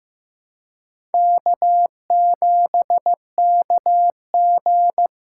Morse code, heard as a keyed tone, about 15 wpm, 700 Hz. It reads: K7KG
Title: USA Call Signs - 4 Characters - 4x Character Spacing - 15wpm